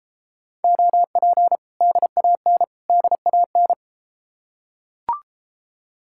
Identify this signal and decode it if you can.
Morse code, heard as a keyed tone, about 32 wpm, 700 Hz. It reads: OP DAN DAN